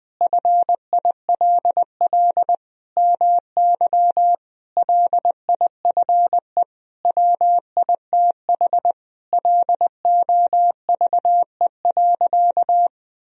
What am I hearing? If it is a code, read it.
Morse code, heard as a keyed tone, about 21 wpm, 700 Hz. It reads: FILL MY LIFE WITH LOVE.
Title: Sentences from Top 1000 Words - Review Part 3 - 20wpm